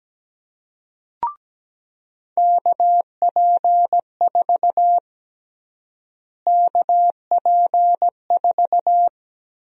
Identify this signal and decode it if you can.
Morse code, heard as a keyed tone, about 17 wpm, 700 Hz. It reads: KP4 KP4
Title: USA Call Sign Prefixes - Repeated 2x - 17wpm